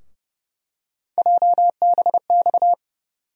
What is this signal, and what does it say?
Morse code, heard as a keyed tone, about 30 wpm, 700 Hz. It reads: JBX
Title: USA Call Sign Postfixes - 3 Letters 30wpm